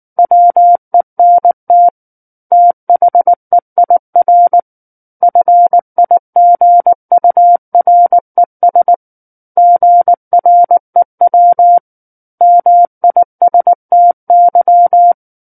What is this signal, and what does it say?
Morse code, heard as a keyed tone, about 19 wpm, 700 Hz. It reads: WENT THEIR FIGURES GREW MISTY
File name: Book One / 10-Chapter 10 - 19 WPM